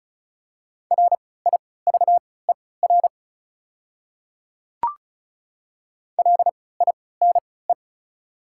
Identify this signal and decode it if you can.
Morse code, heard as a keyed tone, about 36 wpm, spaced as for 20 wpm, 690 Hz. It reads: RIVER LINE
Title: 3x Character Spacing - Top 500 Words - 35wpm